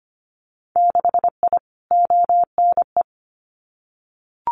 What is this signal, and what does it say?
Morse code, heard as a keyed tone, about 25 wpm, 700 Hz. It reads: 6I ONE
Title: Winter Field Day Exchanges Encore 25wpm